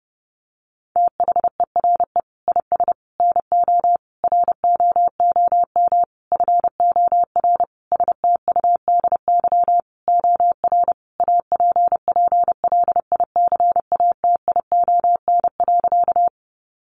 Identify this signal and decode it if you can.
Morse code, heard as a keyed tone, about 30 wpm, 700 Hz. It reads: THERE IS NO ROOM FOR STUDY OR APPLICATION.
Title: Sentences from Top 1200 Words 30wpm